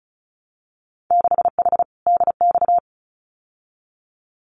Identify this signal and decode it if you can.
Morse code, heard as a keyed tone, about 36 wpm, 700 Hz. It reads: 6H DX
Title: Winter Field Day Exchanges Encore 35wpm